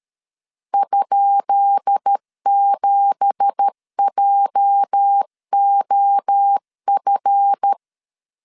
Morse code, heard as a keyed tone, about 13 wpm, 800 Hz. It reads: ?7JOF
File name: LEZIONI / 10 SIGNS / 10B INT 12wpm 2414